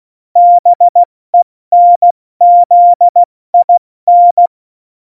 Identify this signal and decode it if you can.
Morse code, heard as a keyed tone, about 16 wpm, 700 Hz. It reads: BENZIN
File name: Morsekurs / Witze / Witz 11 August 2020, 70 F80